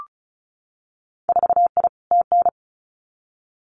Morse code, heard as a keyed tone, about 36 wpm, 700 Hz. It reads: 4I TN
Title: Winter Field Day Exchanges 35wpm